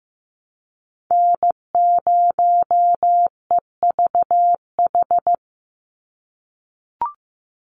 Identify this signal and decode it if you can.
Morse code, heard as a keyed tone, about 15 wpm, 700 Hz. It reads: N0EVH
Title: NA SOTA Chaser Callsigns 15wpm